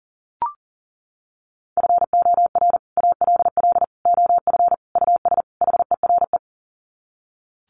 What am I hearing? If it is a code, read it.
Morse code, heard as a keyed tone, about 40 wpm, 700 Hz. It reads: FOR ALL OF US HERE
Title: Sets of 5 Words 40wpm